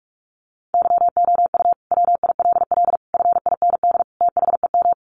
Morse code, heard as a keyed tone, about 44 wpm, 700 Hz. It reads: YOU WILL FIND THEM
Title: Sentences from Top 700 Words 45wpm